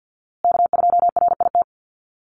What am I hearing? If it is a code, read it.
Morse code, heard as a keyed tone, about 50 wpm, 700 Hz. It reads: K2RIT
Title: USA Call Signs Repeated 2x 50wpm